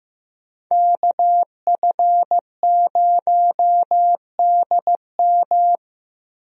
Morse code, heard as a keyed tone, about 15 wpm, 700 Hz. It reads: KF0DM
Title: USA Call Signs - 5 Characters - Repeated 2x - 15wpm